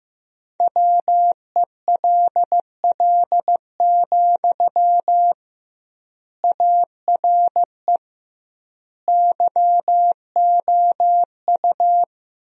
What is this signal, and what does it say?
Morse code, heard as a keyed tone, about 15 wpm, 700 Hz. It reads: WELL, ARE YOU
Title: 2x Word-Spacing - Sentences from Top 400 Words - 15wpm